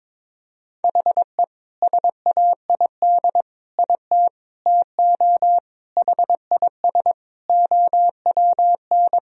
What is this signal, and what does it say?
Morse code, heard as a keyed tone, about 22 wpm, 700 Hz. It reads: HE SAID IT TO HIS OWN
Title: Sentences from Top 1000 Words - Review Part 1 - 22wpm